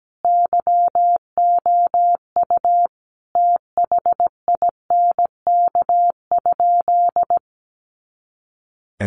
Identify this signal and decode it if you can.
Morse code, heard as a keyed tone, about 17 wpm, 700 Hz. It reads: YOU THINK?
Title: Sentences from Top 200 Words 17wpm